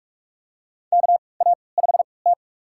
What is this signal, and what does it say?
Morse code, heard as a keyed tone, about 46 wpm, spaced as for 26 wpm, 700 Hz. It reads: KA5T
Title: USA Call Signs - 4 Characters - 3x Character Spacing - 45wpm